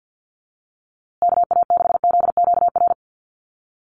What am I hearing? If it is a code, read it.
Morse code, heard as a keyed tone, about 50 wpm, 700 Hz. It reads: KA6ZQR
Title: USA Call Signs Repeated 3x 50wpm